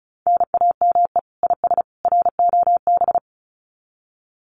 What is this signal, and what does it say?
Morse code, heard as a keyed tone, about 34 wpm, 700 Hz. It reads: NAME IS ROB
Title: Common QSO Phrases 35wpm